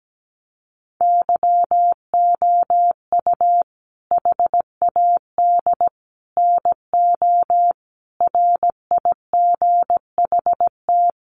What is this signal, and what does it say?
Morse code, heard as a keyed tone, about 17 wpm, 700 Hz. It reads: YOU HAD NO RIGHT
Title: Sentences from Top 1300 Words 17wpm